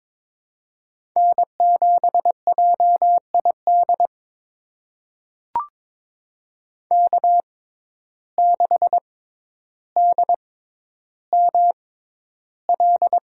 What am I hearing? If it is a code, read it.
Morse code, heard as a keyed tone, about 22 wpm, spaced as for 11 wpm, 700 Hz. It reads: N7JID K6DML
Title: USA Call Signs - 5 Characters - 6x Character Spacing - 22wpm